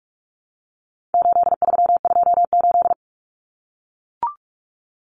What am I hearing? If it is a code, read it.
Morse code, heard as a keyed tone, about 44 wpm, 700 Hz. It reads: 8328
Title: Numbers - 4-Digit 45wpm